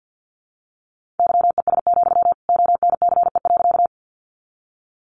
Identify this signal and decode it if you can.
Morse code, heard as a keyed tone, about 50 wpm, 700 Hz. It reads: YES, ONCE.
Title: Sentences from Top 500 Words 50wpm